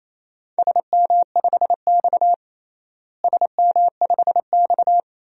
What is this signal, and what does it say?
Morse code, heard as a keyed tone, about 27 wpm, 700 Hz. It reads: SM5X SM5X
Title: DXCC-100 Call Signs - 4 Characters - Repeated 2x - 28wpm